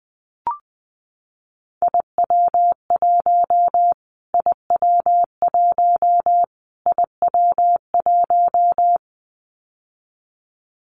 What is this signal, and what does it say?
Morse code, heard as a keyed tone, about 19 wpm, 700 Hz. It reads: IW1 IW1 IW1
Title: Callsign Prefixes for Top 25 DXCC - Repeated 3x - 20wpm